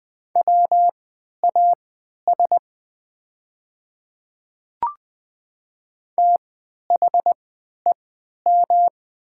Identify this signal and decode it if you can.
Morse code, heard as a keyed tone, about 20 wpm, spaced as for 11 wpm, 700 Hz. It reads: WAS THEM
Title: 3x Character Spacing - Top 200 Words - 20wpm